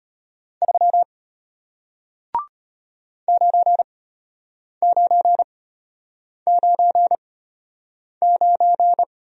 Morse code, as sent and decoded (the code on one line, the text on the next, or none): ...-- / ----. ----. ----. ----.
3 9999